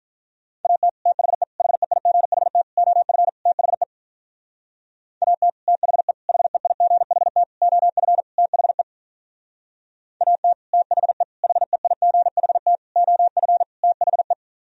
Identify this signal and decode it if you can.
Morse code, heard as a keyed tone, about 43 wpm, 700 Hz. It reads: AT THE HEIGHT OF THE AT THE HEIGHT OF THE AT THE HEIGHT OF THE
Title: Speed-Racing - Sets of 5 Words 35wpm